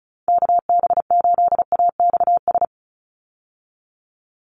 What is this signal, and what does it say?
Morse code, heard as a keyed tone, about 34 wpm, 700 Hz. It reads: KB8AXS
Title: USA Call Signs 35wpm